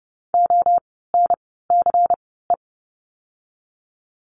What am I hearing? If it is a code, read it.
Morse code, heard as a keyed tone, about 30 wpm, 700 Hz. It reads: O N C E
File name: Lesson 16f - Letter c - Words 30wpm